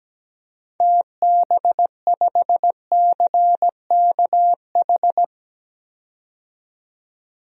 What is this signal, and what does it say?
Morse code, heard as a keyed tone, about 17 wpm, 700 Hz. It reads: TB5CKH
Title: DXCC-100 Call Signs - 6 Characters - 17wpm